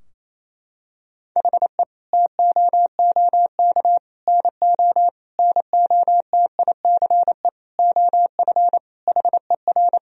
Morse code, heard as a keyed tone, about 28 wpm, 700 Hz. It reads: HE TOOK NO NOTICE OF HER
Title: Sentences from Top 1000 Words - Review Part 2 - 28wpm